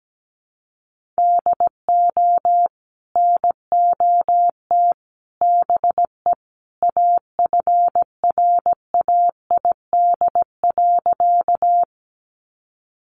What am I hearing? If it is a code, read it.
Morse code, heard as a keyed tone, about 17 wpm, 700 Hz. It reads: DO NOT BE AFRAID.
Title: Sentences from Top 1500 Words 17wpm